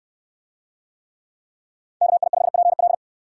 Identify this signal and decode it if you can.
Morse code, heard as a keyed tone, about 64 wpm, 700 Hz. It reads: KI5PL